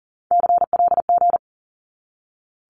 Morse code, heard as a keyed tone, about 40 wpm, 700 Hz. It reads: CLG
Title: Abbreviations 40wpm